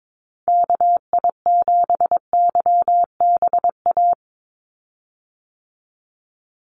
Morse code, heard as a keyed tone, about 22 wpm, 700 Hz. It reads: KI7YBA
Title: USA Call Signs 22wpm